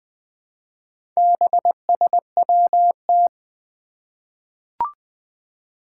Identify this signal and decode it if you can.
Morse code, heard as a keyed tone, about 20 wpm, 700 Hz. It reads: BSWT
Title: Four Challenging Letter-Number Combinations 20wpm